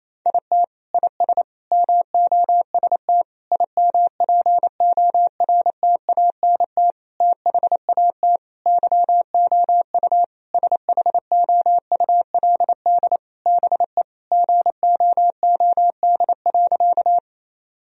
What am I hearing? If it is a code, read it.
Morse code, heard as a keyed tone, about 28 wpm, 700 Hz. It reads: IT IS MOST IMPORTANT THAT YOU SHOULD BE GOOD.
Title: Sentences from Top 1000 Words - Review Part 1 - 28wpm